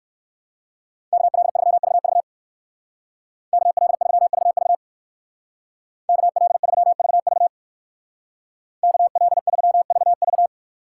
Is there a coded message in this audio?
Yes: Morse code from 1.1 to 10.5 s, heard as a keyed tone, about 59 wpm, 700 Hz. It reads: KL3VV KL3VV KL3VV KL3VV